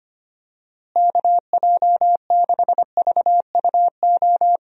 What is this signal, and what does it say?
Morse code, heard as a keyed tone, about 25 wpm, 700 Hz. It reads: KJ6VUO
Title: USA Call Signs - 6 Characters - Repeated 3x - 25wpm